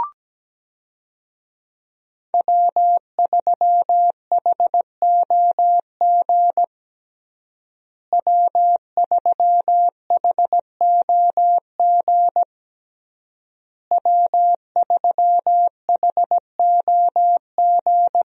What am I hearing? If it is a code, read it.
Morse code, heard as a keyed tone, about 17 wpm, 700 Hz. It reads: W3HOG W3HOG W3HOG
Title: USA Call Signs - 5 Characters - Repeated 3x - 17wpm